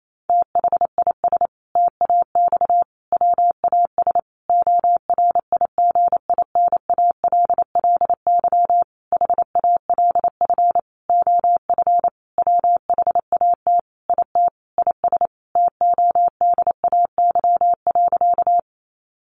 Morse code, heard as a keyed tone, about 28 wpm, 700 Hz. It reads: THIS TAX WAS ORIGINALLY HALF OF WHAT IT IS TODAY.
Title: Sentences from Top 1500 Words 28wpm